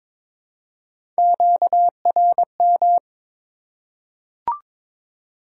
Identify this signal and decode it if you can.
Morse code, heard as a keyed tone, about 22 wpm, 700 Hz. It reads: QRM
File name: Top 90 QSO Elements - 4x Character Spacing - 22wpm